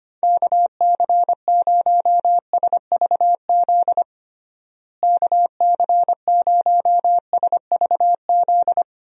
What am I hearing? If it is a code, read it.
Morse code, heard as a keyed tone, about 25 wpm, 700 Hz. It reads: KC0SVZ KC0SVZ
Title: USA Call Signs - 6 Characters - Repeated 3x - 25wpm